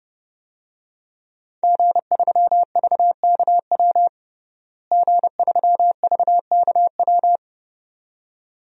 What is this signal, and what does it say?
Morse code, heard as a keyed tone, about 30 wpm, 700 Hz. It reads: G3VKW G3VKW
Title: DXCC-100 Call Signs - 5 Characters - Repeated 2x - 30wpm